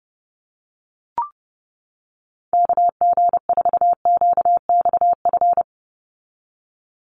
Morse code, heard as a keyed tone, about 30 wpm, 700 Hz. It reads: KG4QXF
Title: USA Call Signs 30wpm